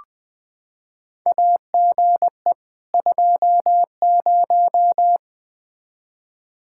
Morse code, heard as a keyed tone, about 20 wpm, 700 Hz. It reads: AGE 20
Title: Practice QSO - Exchange 3 - More Info - 20wpm